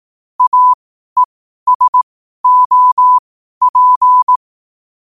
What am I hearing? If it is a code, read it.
Morse code, heard as a keyed tone, about 18 wpm, spaced as for 12 wpm, 1000 Hz. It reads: AESOP